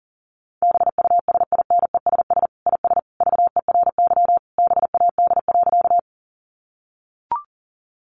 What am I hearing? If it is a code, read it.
Morse code, heard as a keyed tone, about 40 wpm, 700 Hz. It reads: BUSINESS IS VERY BAD.
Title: Sentences from Top 700 Words 40wpm